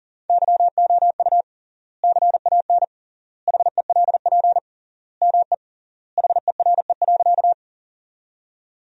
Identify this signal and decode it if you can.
Morse code, heard as a keyed tone, about 37 wpm, 700 Hz. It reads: YOU CAN HELP ME HERE.
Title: 3x Word-Spacing - Sentences from Top 200 Words - 40wpm